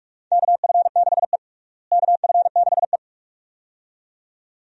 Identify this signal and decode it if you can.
Morse code, heard as a keyed tone, about 42 wpm, 700 Hz. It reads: KF6E KF6E
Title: USA Call Signs - 4 Characters - Repeated 2x - 45wpm